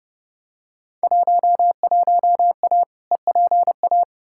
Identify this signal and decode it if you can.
Morse code, heard as a keyed tone, about 32 wpm, 700 Hz. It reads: 11A EPA
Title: Field Day Exchanges Encore - 30wpm